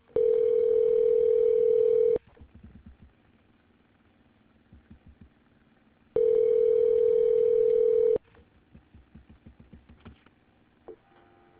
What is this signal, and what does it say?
Ringback tone (North America), 440 + 480 Hz.